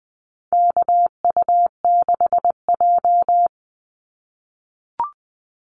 Morse code, heard as a keyed tone, about 20 wpm, 700 Hz. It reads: KU6J
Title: NA SOTA Activator Callsigns 20wpm